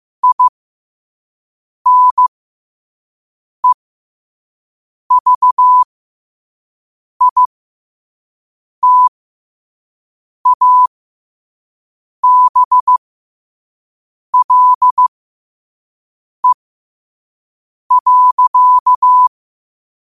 Morse code, heard as a keyed tone, about 15 wpm, spaced as for 5 wpm, 1000 Hz. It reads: INEVITABLE.
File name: qotdfiles / 5wpm